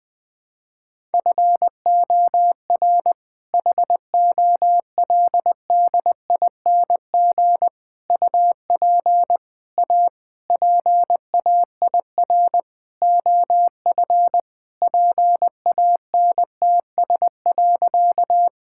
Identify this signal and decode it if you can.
Morse code, heard as a keyed tone, about 20 wpm, 700 Hz. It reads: FOR HOLDING UP A PAIR OF PANTS.
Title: All-Age Appropriate Jokes - 20wpm